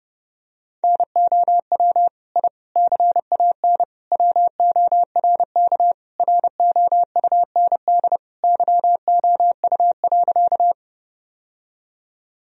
Morse code, heard as a keyed tone, about 30 wpm, 700 Hz. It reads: NOW I CAN WORK ROUND YOU.
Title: Sentences from Top 1000 Words - Review Part 1 - 30wpm